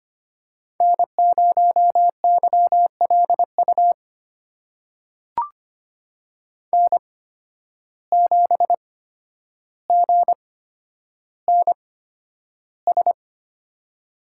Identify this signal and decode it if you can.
Morse code, heard as a keyed tone, about 25 wpm, spaced as for 11 wpm, 700 Hz. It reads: N0YLU N7GNS
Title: USA Call Signs - 5 Characters - 8x Character Spacing - 25wpm